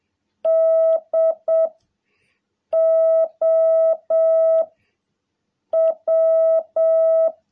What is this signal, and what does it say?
Morse code, heard as a keyed tone, about 7 wpm, 650 Hz. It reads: D O W
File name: jh8cbh / 2021cw